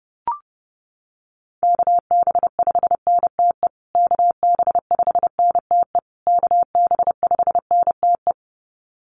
Morse code, heard as a keyed tone, about 30 wpm, 700 Hz. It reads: KB5NTE KB5NTE KB5NTE
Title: USA Call Signs Repeated 3x 30wpm